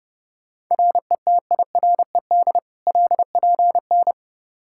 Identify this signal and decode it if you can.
Morse code, heard as a keyed tone, about 30 wpm, 700 Hz. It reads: RETIRED LPN
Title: Practice QSO - Exchange 3 - More Info - 30wpm